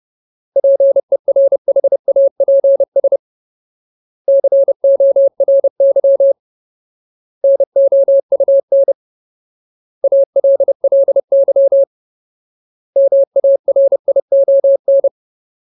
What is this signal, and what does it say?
Morse code, heard as a keyed tone, about 30 wpm, 550 Hz. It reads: PERHAPS CORY NOUN ALLY MARION
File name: Ultimate Word List / 30WPM Ultimate